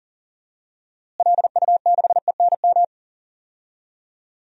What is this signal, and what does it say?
Morse code, heard as a keyed tone, about 40 wpm, 700 Hz. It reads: LU6ENM
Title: DXCC-100 Call Signs - 6 Characters - Repeated 3x - 40wpm